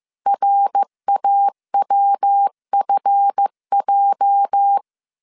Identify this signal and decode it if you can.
Morse code, heard as a keyed tone, about 15 wpm, 800 Hz. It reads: RAWFJ
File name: CW / DET / 1 9 1500C S C 14wpm 3245